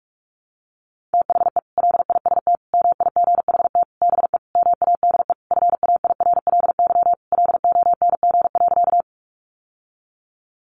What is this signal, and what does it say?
Morse code, heard as a keyed tone, about 44 wpm, 700 Hz. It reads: THE LIST MIGHT BE MADE FAIRLY LONG.
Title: Sentences from Top 1100 Words 45wpm